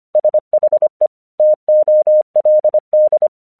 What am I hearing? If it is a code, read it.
Morse code, heard as a keyed tone, about 25 wpm, 610 Hz. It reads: SHE TOLD
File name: 1 English 25wpm / AQC14